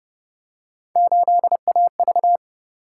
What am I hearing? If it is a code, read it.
Morse code, heard as a keyed tone, about 32 wpm, 700 Hz. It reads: 8AV